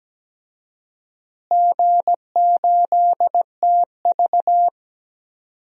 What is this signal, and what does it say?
Morse code, heard as a keyed tone, about 17 wpm, 700 Hz. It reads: G8TV
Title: DXCC-100 Call Signs - 4 Characters - Repeated 3x - 17wpm